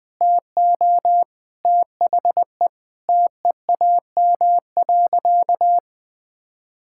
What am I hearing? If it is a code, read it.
Morse code, heard as a keyed tone, about 21 wpm, 700 Hz. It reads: TO THE TEAM.
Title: Sentences with 6-Word Phrases - 20wpm